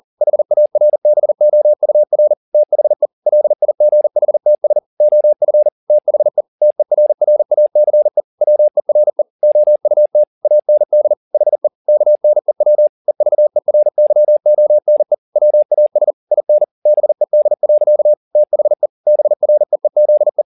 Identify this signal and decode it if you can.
Morse code, heard as a keyed tone, about 40 wpm, 600 Hz. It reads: HARBOUR THE LIGHTS OF THE TERRACE WERE OUT AND HE KNEW EVERYONE WAS IN BED. THE BREEZE
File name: the old man and the sea 2 40wpm